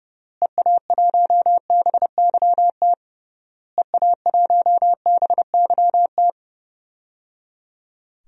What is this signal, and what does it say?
Morse code, heard as a keyed tone, about 30 wpm, 700 Hz. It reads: EA1BYT EA1BYT
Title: DXCC-100 Call Signs - 6 Characters - Repeated 3x - 30wpm